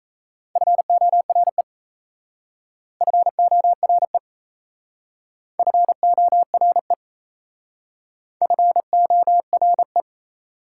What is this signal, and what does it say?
Morse code, heard as a keyed tone, about 35 wpm, 690 Hz. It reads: FORE FORE FORE FORE
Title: Speed-Racing Prefixes and Suffixes 28wpm